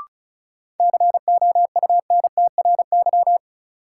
Morse code, heard as a keyed tone, about 36 wpm, 700 Hz. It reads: COUNTRY
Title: Sets of 2 Words - New Word Drill - 35wpm